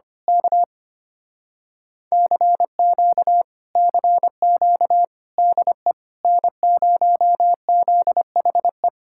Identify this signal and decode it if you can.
Morse code, heard as a keyed tone, about 24 wpm, 700 Hz. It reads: K CQ CQ DE N0ZHE
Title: Speed-Racing Common QSO Phrases - 25wpm